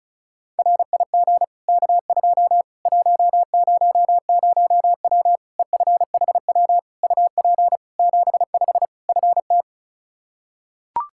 Morse code, heard as a keyed tone, about 34 wpm, 700 Hz. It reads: RIG K2 100W EFHW UP 75 FT
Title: Practice QSO - Exchange 2 - Info - 35wpm